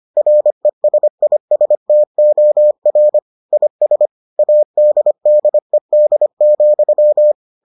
Morse code, heard as a keyed tone, about 25 wpm, 600 Hz. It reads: RESISTOR IS ADDED,